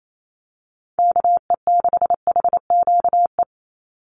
Morse code, heard as a keyed tone, about 28 wpm, 700 Hz. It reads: KE6HQE
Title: USA Call Signs 28wpm